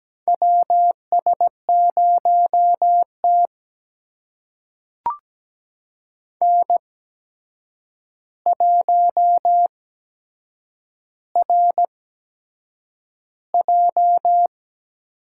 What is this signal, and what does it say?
Morse code, heard as a keyed tone, about 17 wpm, spaced as for 7 wpm, 700 Hz. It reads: WS0T N1RJ
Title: USA Call Signs - 4 Characters - 8x Character Spacing - 17wpm